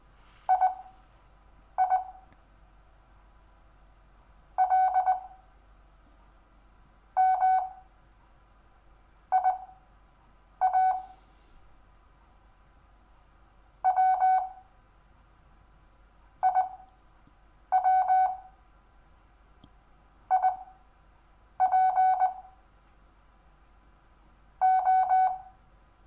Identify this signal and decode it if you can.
Morse code, heard as a keyed tone, about 20 wpm, spaced as for 7 wpm, 750 Hz. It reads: II L M IA W IW IP O
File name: cw / Lesson6 WI